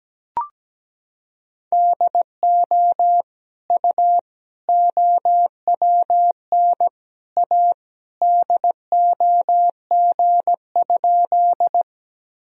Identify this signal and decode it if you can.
Morse code, heard as a keyed tone, about 17 wpm, 700 Hz. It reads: DO U OWN A DOG?
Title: Sentences from Top 400 Words and Abbreviations 17wpm